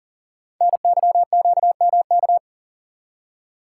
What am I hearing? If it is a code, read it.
Morse code, heard as a keyed tone, about 41 wpm, 700 Hz. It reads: NYQMK
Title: Code Groups - 5 Letters - 40wpm